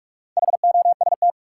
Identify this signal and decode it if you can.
Morse code, heard as a keyed tone, about 46 wpm, 700 Hz. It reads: HOST
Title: Four Letter Words - 45wpm